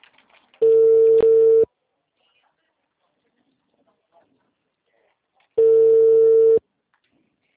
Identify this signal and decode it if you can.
Ringback tone (China, 450 Hz).